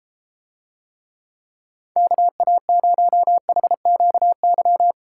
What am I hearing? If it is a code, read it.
Morse code, heard as a keyed tone, about 33 wpm, 700 Hz. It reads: KA0HQY